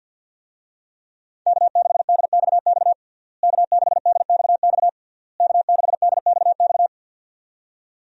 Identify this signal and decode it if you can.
Morse code, heard as a keyed tone, about 51 wpm, 700 Hz. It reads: K6DXX K6DXX K6DXX
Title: USA Call Signs - 5 Characters - Repeated 3x - 50wpm